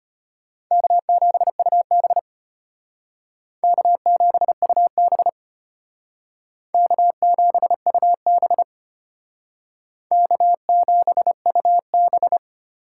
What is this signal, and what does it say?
Morse code, heard as a keyed tone, about 32 wpm, 700 Hz. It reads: K7UB K7UB K7UB K7UB